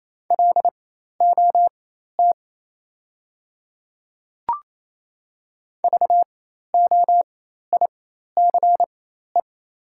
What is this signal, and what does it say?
Morse code, heard as a keyed tone, about 28 wpm, spaced as for 13 wpm, 700 Hz. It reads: LOT VOICE
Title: 4x Character Spacing - Top 500 Words - 28wpm